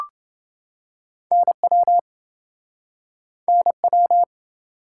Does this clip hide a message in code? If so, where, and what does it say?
Yes: Morse code from 1.3 to 4.2 s, heard as a keyed tone, about 29 wpm, 700 Hz. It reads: NW NW